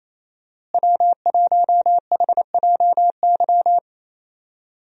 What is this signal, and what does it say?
Morse code, heard as a keyed tone, about 27 wpm, 700 Hz. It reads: W1HJY
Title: USA Call Signs - 5 Characters - Repeated 2x - 28wpm